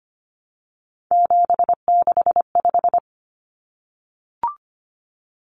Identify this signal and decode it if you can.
Morse code, heard as a keyed tone, about 25 wpm, 700 Hz. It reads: 765
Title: Numbers - 3-Digit 25wpm